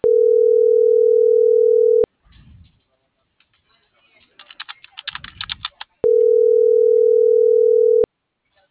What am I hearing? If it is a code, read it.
Ringback tone (North America), 440 + 480 Hz.